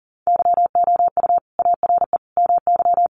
Morse code, heard as a keyed tone, about 40 wpm, 700 Hz. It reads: YOU ARE MY
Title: Sentences from Top 1400 Words 40wpm